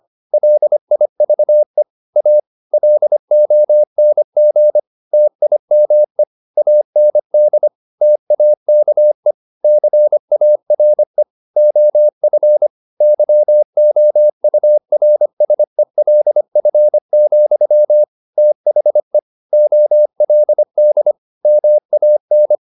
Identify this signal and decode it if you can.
Morse code, heard as a keyed tone, about 25 wpm, 600 Hz. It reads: LIVE A LONG TIME AND TAKE CARE OF YOURSELF, THE OLD MAN
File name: the old man and the sea part 1 25wpm